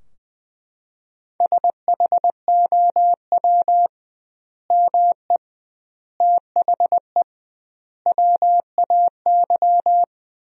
Morse code, heard as a keyed tone, about 20 wpm, 700 Hz. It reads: SHOW ME THE WAY
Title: 2x Word-Spacing - Sentences from Top 200 Words - 20wpm